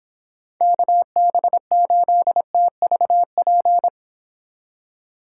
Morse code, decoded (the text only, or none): KB8TVP